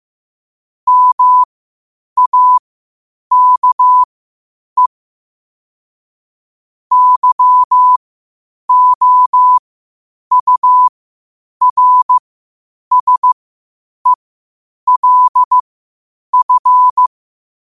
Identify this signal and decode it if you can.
Morse code, heard as a keyed tone, about 15 wpm, spaced as for 8 wpm, 1000 Hz. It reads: MAKE YOURSELF
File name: qotdfiles / 8wpm